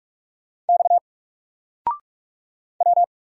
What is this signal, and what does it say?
Morse code, heard as a keyed tone, about 45 wpm, 700 Hz. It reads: X W
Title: Single Character (Letter:Number:Punctuation:Prosign) 45wpm